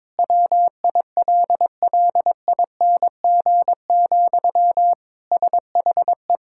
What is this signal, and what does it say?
Morse code, heard as a keyed tone, about 22 wpm, 700 Hz. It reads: WILLING, SHE
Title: Sentences from Top 1600 Words 22wpm